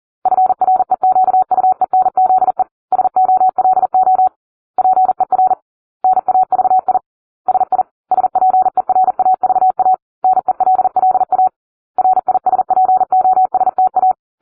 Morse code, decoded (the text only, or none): FREQUENZE SOLO PER NAVI SI SPERAVA NELLA RISPOSTA